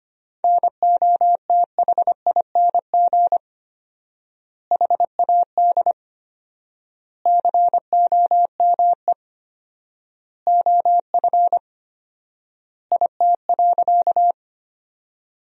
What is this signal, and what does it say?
Morse code, heard as a keyed tone, about 25 wpm, 700 Hz. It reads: NOTHING HAD COME OF IT.
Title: 4x Word-Spacing - Sentences from Top 500 Words - 25wpm